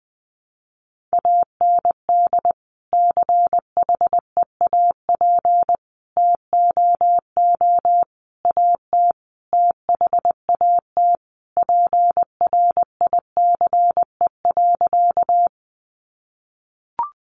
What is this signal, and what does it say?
Morse code, heard as a keyed tone, about 20 wpm, 700 Hz. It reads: AND CHEAP TOO AT THAT PRICE.
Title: Sentences from Top 1200 Words 20wpm